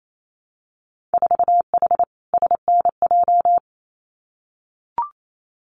Morse code, heard as a keyed tone, about 28 wpm, 700 Hz. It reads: 4H SNJ
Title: Winter Field Day Exchanges Encore 28wpm